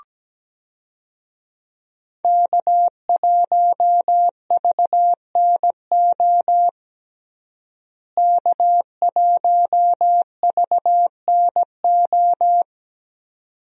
Morse code, heard as a keyed tone, about 17 wpm, 700 Hz. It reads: K1VNO K1VNO
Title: USA Call Signs - 5 Characters - Repeated 2x - 17wpm